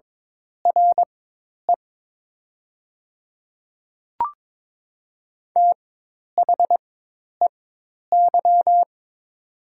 Morse code, heard as a keyed tone, about 22 wpm, spaced as for 10 wpm, 700 Hz. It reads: RE THEY
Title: 4x Character Spacing - Top 200 Words - 22wpm